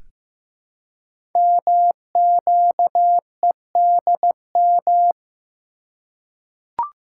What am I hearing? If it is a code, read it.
Morse code, heard as a keyed tone, about 15 wpm, 700 Hz. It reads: MQEDM
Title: Code Groups - 5 Letter:Numbers - 15wpm